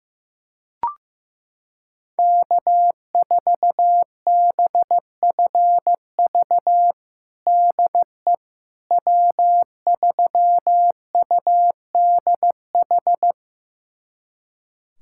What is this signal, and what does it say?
Morse code, heard as a keyed tone, about 16 wpm, 700 Hz. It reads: K4BFV DE W3UDH
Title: USA Callsign DE Callsign - 5 Characters - 15wpm